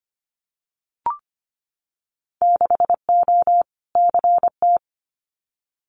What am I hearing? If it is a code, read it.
Morse code, heard as a keyed tone, about 25 wpm, 700 Hz. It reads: 6O CT
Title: Winter Field Day Exchanges Encore 25wpm